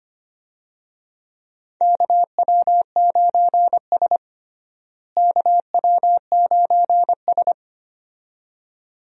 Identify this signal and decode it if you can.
Morse code, heard as a keyed tone, about 25 wpm, 700 Hz. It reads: KW9S KW9S
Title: USA Call Signs - 4 Characters - Repeated 2x - 25wpm